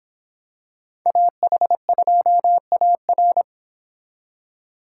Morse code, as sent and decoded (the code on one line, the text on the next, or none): .- .... ..--- .- .-.
AH2AR